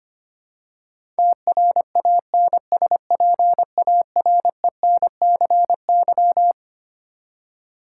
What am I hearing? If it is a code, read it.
Morse code, heard as a keyed tone, about 25 wpm, 700 Hz. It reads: TRANSPARENCY
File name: Twelve Letter Words - 25wpm